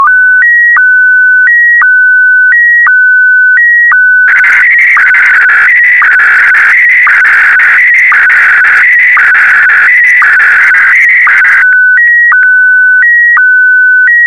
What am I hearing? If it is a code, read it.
SSTV (slow-scan television), Scottie DX.